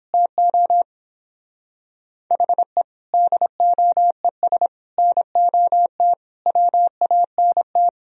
Morse code, heard as a keyed tone, about 29 wpm, 700 Hz. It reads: TO HE DOES NOT WANT